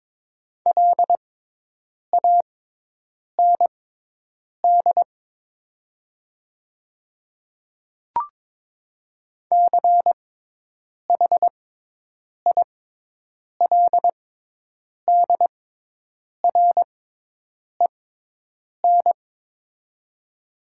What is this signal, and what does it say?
Morse code, heard as a keyed tone, about 21 wpm, spaced as for 8 wpm, 700 Hz. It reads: LAND CHILDREN
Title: 6x Character Spacing - Top 300 Words - 22wpm